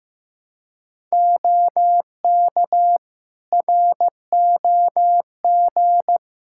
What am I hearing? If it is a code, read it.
Morse code, heard as a keyed tone, about 15 wpm, 700 Hz. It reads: OK ROG
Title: Practice QSO - Exchange 3 - More Info - 15wpm